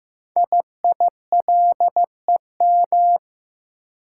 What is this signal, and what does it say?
Morse code, heard as a keyed tone, about 15 wpm, 700 Hz. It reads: IILEM
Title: Code Groups - 5 Letter:Numbers - 15wpm